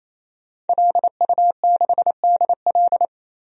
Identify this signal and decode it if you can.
Morse code, heard as a keyed tone, about 27 wpm, 700 Hz. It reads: LU6DL
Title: DXCC-100 Call Signs - 5 Characters - 28wpm